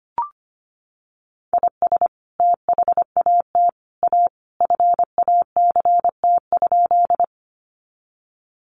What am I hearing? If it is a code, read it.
Morse code, heard as a keyed tone, about 25 wpm, 700 Hz. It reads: IS THAT A FACT?
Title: Sentences from Top 500 Words 25wpm